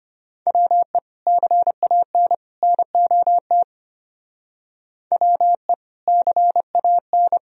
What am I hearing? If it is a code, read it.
Morse code, heard as a keyed tone, about 33 wpm, 700 Hz. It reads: WE CAN NOT WE CAN